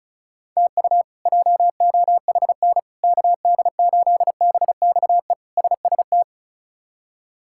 Morse code, heard as a keyed tone, about 34 wpm, 700 Hz. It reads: TU JOHN KD8BXE SST
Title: K1USN SST QSO Practice - 35wpm